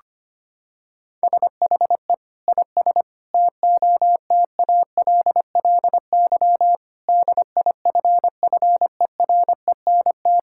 Morse code, heard as a keyed tone, about 25 wpm, 700 Hz. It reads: SHE IS TOTALLY DIFFERENT
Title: Sentences from Top 1600 Words 25wpm